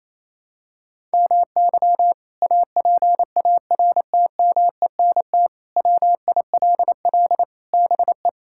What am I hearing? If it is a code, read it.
Morse code, heard as a keyed tone, about 28 wpm, 700 Hz. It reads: MY APARTMENT WILL BE
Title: Sentences from Top 1000 Words - Review Part 6 - 28wpm